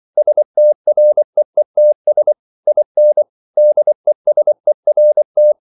Morse code, heard as a keyed tone, about 24 wpm, 600 Hz. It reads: STREETS IN DESERT